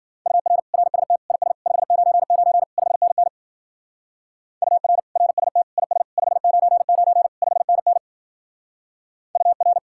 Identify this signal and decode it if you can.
Morse code, heard as a keyed tone, about 48 wpm, 700 Hz. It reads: UR RST IS 599 5NN UR RST IS 599 5NN UR
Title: Speed-Racing Common QSO Phrases - 40wpm